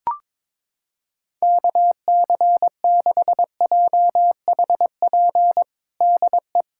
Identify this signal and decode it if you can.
Morse code, heard as a keyed tone, about 22 wpm, 700 Hz. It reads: KC6JHP DE
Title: USA Callsign DE Callsign 22wpm